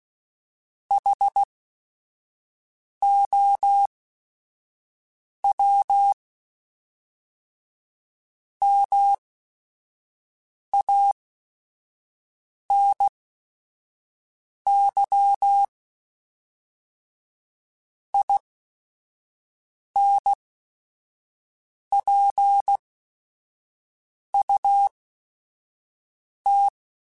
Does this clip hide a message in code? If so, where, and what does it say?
Yes: Morse code from 0.9 to 26.7 s, heard as a keyed tone, about 16 wpm, spaced as for 5 wpm, 800 Hz. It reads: HOW MANY INPUT